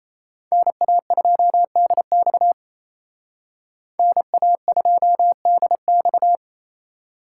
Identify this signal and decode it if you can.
Morse code, heard as a keyed tone, about 35 wpm, 700 Hz. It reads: NA2DX NA2DX